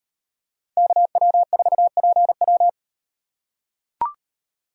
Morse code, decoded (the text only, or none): KW4PW